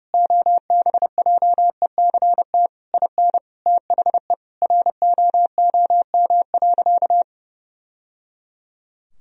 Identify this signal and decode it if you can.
Morse code, heard as a keyed tone, about 32 wpm, 700 Hz. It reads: OBJECT IN THE ROOM.
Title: Sentences from Top 1000 Words - Review Part 3 - 30wpm